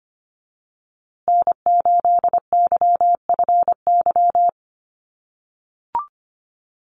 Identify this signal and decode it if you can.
Morse code, heard as a keyed tone, about 25 wpm, 700 Hz. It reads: N8YFY
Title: USA Call Signs Repeated 2x 25wpm